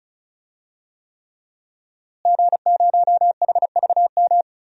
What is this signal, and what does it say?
Morse code, heard as a keyed tone, about 35 wpm, 700 Hz. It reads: G0HVM
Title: DXCC-100 Call Signs - 5 Characters - Repeated 3x - 35wpm